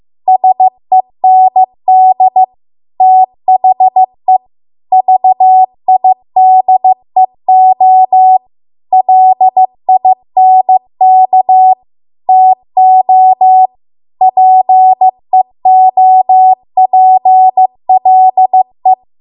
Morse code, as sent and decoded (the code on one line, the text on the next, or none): ... -. -.. / - .... / ...- .. -.. --- / .-.. .. -. -.- / - --- / .--. --- .--. .-.. .
SEND THE VIDEO LINK TO PEOPLE